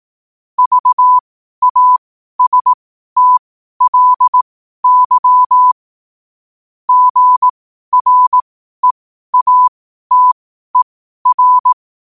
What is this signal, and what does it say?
Morse code, heard as a keyed tone, about 18 wpm, spaced as for 12 wpm, 1000 Hz. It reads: VASTLY GREATER
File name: qotdfiles / 12wpm